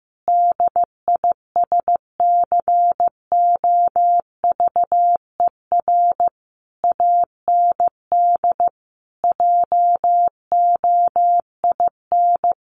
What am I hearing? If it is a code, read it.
Morse code, heard as a keyed tone, about 15 wpm, 700 Hz. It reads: DISCOVER AND JOIN
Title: Sentences from Top 1400 Words 15wpm